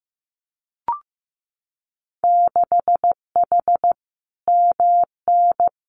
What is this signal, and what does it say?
Morse code, heard as a keyed tone, about 15 wpm, 700 Hz. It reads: 6H MN
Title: Winter Field Day Exchanges Encore 15wpm